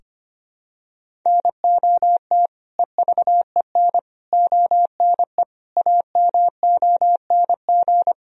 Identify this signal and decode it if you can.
Morse code, heard as a keyed tone, about 25 wpm, 700 Hz. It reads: NOT EVEN ONE AMONG
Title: Sentences from Top 1000 Words - Review Part 3 - 25wpm